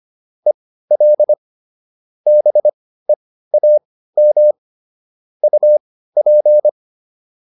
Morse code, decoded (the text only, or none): EL BEAM UP